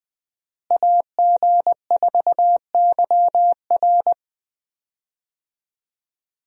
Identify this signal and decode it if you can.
Morse code, heard as a keyed tone, about 20 wpm, 700 Hz. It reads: AG4YR
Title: USA Call Signs - 5 Characters - Repeated 2x - 20wpm